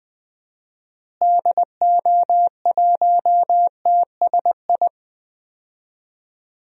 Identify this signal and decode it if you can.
Morse code, heard as a keyed tone, about 20 wpm, 700 Hz. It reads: DO1TSI
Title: DXCC-100 Call Signs - 6 Characters - Repeated 3x - 20wpm